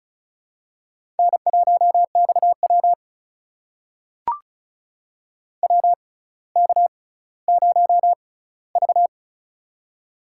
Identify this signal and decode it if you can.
Morse code, heard as a keyed tone, about 36 wpm, spaced as for 18 wpm, 700 Hz. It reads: N1XW WK0V
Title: USA Call Signs - 4 Characters - 6x Character Spacing - 35wpm